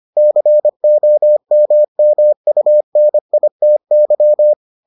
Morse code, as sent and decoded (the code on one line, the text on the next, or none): -.-. --- -- -- ..- -. .. - -.--
COMMUNITY